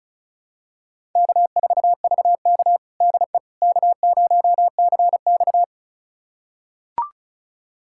Morse code, heard as a keyed tone, about 36 wpm, 700 Hz. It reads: K4VK DE K0CX
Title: USA Callsign DE Callsign - 4 Characters - 35wpm